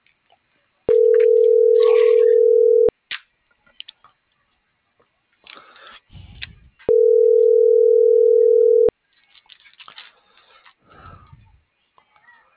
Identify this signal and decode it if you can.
Ringback tone (North America), 440 + 480 Hz.